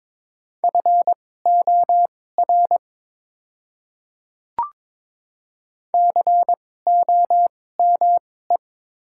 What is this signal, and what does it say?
Morse code, heard as a keyed tone, about 22 wpm, spaced as for 16 wpm, 700 Hz. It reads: FOR COME
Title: 2x Character Spacing - Top 100 Words - 22wpm